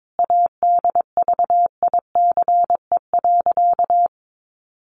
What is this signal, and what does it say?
Morse code, heard as a keyed tone, about 22 wpm, 700 Hz. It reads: ADVICE.
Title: Sentences from Top 1100 Words 22wpm